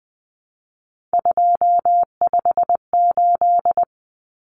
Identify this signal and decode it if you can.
Morse code, heard as a keyed tone, about 20 wpm, 700 Hz. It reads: 258
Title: Numbers - 3-Digit 20wpm